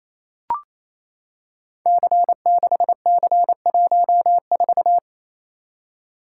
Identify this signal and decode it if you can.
Morse code, heard as a keyed tone, about 28 wpm, 700 Hz. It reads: C6C14
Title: Code Groups - 5 Letter:Numbers - 28wpm